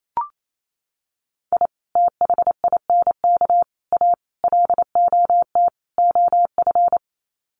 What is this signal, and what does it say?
Morse code, heard as a keyed tone, about 28 wpm, 700 Hz. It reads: I THINK A LOT OF
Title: Sets of 5 Words 28wpm